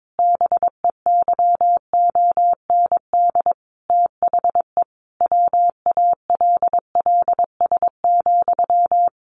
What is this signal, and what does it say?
Morse code, heard as a keyed tone, about 22 wpm, 700 Hz. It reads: BEYOND THE WALLS,
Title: A Sound Of Thunder 22wpm